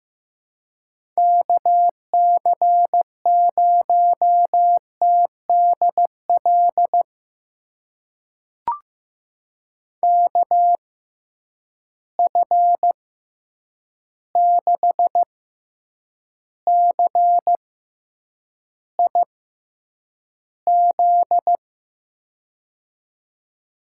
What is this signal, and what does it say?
Morse code, heard as a keyed tone, about 15 wpm, spaced as for 8 wpm, 700 Hz. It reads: KC0TDL KF6CIZ
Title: USA Call Signs - 6 Characters - 6x Character Spacing - 15wpm